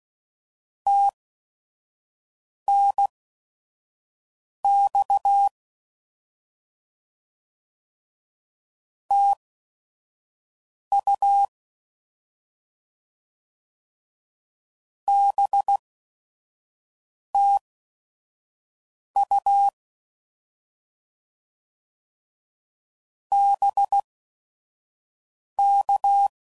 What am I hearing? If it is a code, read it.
Morse code, heard as a keyed tone, about 16 wpm, spaced as for 5 wpm, 800 Hz. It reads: TNX TU BTU BK
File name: Mixed / Misc / Words